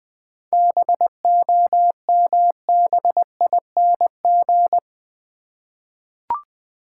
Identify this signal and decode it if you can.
Morse code, heard as a keyed tone, about 21 wpm, 700 Hz. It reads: BOMBING
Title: 7 Letter Words - First 3 Letters Given - 20wpm